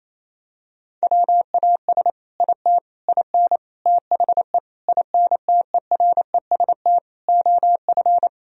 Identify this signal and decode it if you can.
Morse code, heard as a keyed tone, about 28 wpm, 700 Hz. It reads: WAS IT IN THE INTEREST OF
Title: Sentences from Top 1000 Words - Review Part 4 - 28wpm